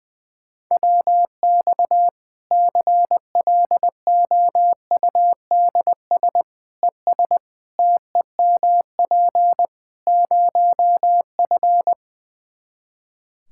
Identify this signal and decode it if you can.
Morse code, heard as a keyed tone, about 21 wpm, 700 Hz. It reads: WX CLOUDS ES TEMP 0F
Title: Practice QSO - Exchange 2 - Info - 20wpm